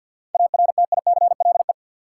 Morse code, heard as a keyed tone, about 53 wpm, 700 Hz. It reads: ARTICLE
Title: Sentences from Top 1000 Words - Review Part 5 - New Word Drill - 50wpm